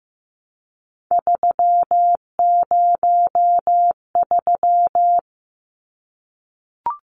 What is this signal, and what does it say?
Morse code, heard as a keyed tone, about 15 wpm, 700 Hz. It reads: 303
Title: Numbers - 3-Digit 15wpm